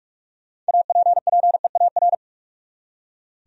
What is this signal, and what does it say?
Morse code, heard as a keyed tone, about 58 wpm, 700 Hz. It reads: APPEAR